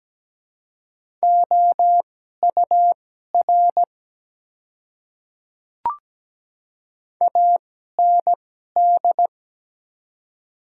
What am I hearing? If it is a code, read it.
Morse code, heard as a keyed tone, about 17 wpm, spaced as for 12 wpm, 700 Hz. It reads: OUR AND
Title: 2x Character Spacing - Top 300 Words - 17wpm